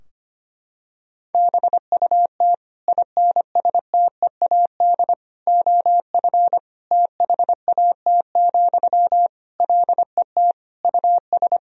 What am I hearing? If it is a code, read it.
Morse code, heard as a keyed tone, about 25 wpm, 700 Hz. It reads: BUT INSTEAD OF THAT, LET US
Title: Sentences from Top 1000 Words - Review Part 4 - 25wpm